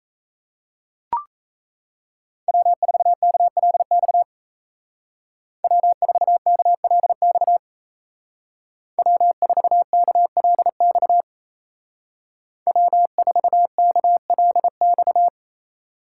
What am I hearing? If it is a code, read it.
Morse code, heard as a keyed tone, about 33 wpm, 700 Hz. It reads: W4KLX W4KLX W4KLX W4KLX
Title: Speed Racing - USA 1x3 Call Signs 28wpm